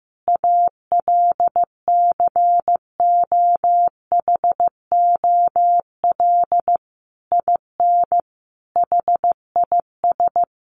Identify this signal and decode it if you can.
Morse code, heard as a keyed tone, about 15 wpm, 700 Hz. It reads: ALCOHOL IN HIS
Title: Sentences from Top 1400 Words 15wpm